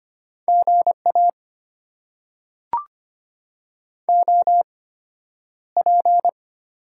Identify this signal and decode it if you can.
Morse code, heard as a keyed tone, about 25 wpm, spaced as for 11 wpm, 700 Hz. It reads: GA OP
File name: Top 90 QSO Elements - 8x Character Spacing - 25wpm